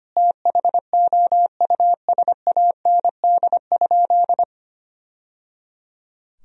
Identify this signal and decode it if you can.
Morse code, heard as a keyed tone, about 25 wpm, 700 Hz. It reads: THOUSAND?
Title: Sentences from Top 1000 Words - Review Part 3 - 25wpm